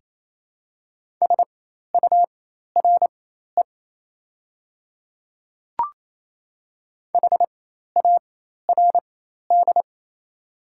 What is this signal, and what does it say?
Morse code, heard as a keyed tone, about 27 wpm, spaced as for 13 wpm, 700 Hz. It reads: SURE HARD
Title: 4x Character Spacing - Top 300 Words - 28wpm